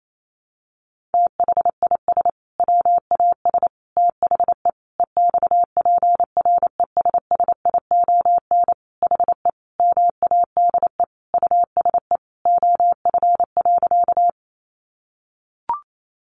Morse code, heard as a keyed tone, about 28 wpm, 700 Hz. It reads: THIS WAS THE EXPRESSION HE MADE USE OF.
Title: Sentences from Top 1200 Words 28wpm